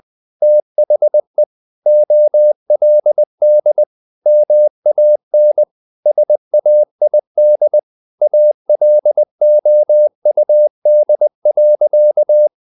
Morse code, heard as a keyed tone, about 20 wpm, 600 Hz. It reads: THE OLD MAN SAID ALOUD.
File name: the old man and the sea part 2 20wpm